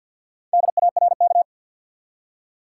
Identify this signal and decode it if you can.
Morse code, heard as a keyed tone, about 52 wpm, 700 Hz. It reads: DARK
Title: Four Letter Words - 50wpm